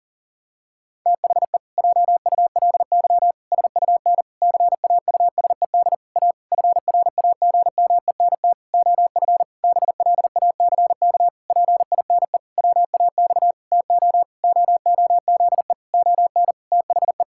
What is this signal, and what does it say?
Morse code, heard as a keyed tone, about 37 wpm, 700 Hz. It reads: THE JULY SUN CAUSED A FRAGMENT OF BLACK PINE WAX TO OOZE ON THE
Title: Pangram Challenge v2 - 40wpm